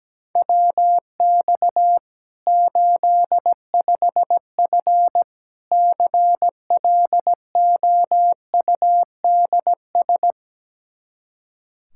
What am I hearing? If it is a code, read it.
Morse code, heard as a keyed tone, about 17 wpm, 700 Hz. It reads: WX 85F CLOUDS
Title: Practice QSO - Exchange 2 - Info - 17wpm